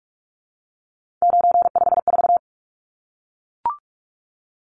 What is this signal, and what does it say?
Morse code, heard as a keyed tone, about 44 wpm, 700 Hz. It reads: 954
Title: Numbers - 3-Digit 45wpm